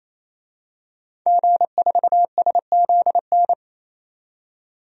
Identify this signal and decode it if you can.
Morse code, heard as a keyed tone, about 27 wpm, 700 Hz. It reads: G4SZN
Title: DXCC-100 Call Signs - 5 Characters - 28wpm